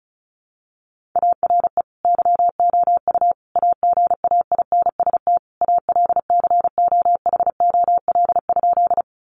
Morse code, heard as a keyed tone, about 34 wpm, 700 Hz. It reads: ARE YOU AGAINST ALCOHOL?
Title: Sentences from Top 1400 Words 35wpm